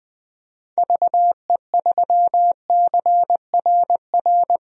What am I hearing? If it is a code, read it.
Morse code, heard as a keyed tone, about 20 wpm, 700 Hz. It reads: VE3CRR
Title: DXCC-100 Call Signs - 6 Characters - Repeated 3x - 20wpm